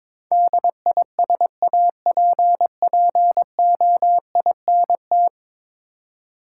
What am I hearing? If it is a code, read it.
Morse code, heard as a keyed tone, about 22 wpm, 700 Hz. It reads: DISAPPOINT
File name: practice / 22wpm / 10 Letter Words - First 3 Letters Given - 22wpm